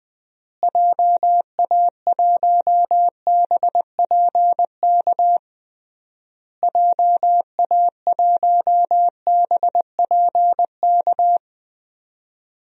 Morse code, heard as a keyed tone, about 20 wpm, 700 Hz. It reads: JA1BPK JA1BPK
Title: DXCC-100 Call Signs - 6 Characters - Repeated 3x - 20wpm